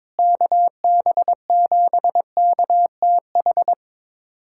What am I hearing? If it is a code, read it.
Morse code, heard as a keyed tone, about 22 wpm, 700 Hz. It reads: KB7KTH
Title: USA Callsign DE Callsign 22wpm